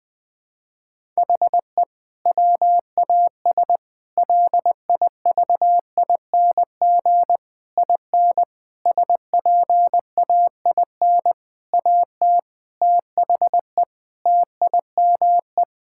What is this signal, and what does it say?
Morse code, heard as a keyed tone, about 20 wpm, 700 Hz. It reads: HE WAS LIVING IN SPAIN AT THE TIME
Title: Sentences with 6-Word Phrases - 20wpm